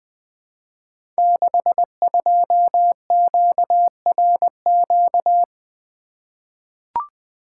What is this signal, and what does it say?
Morse code, heard as a keyed tone, about 20 wpm, 700 Hz. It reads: 62QRQ
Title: Code Groups - 5 Letter:Numbers - 20wpm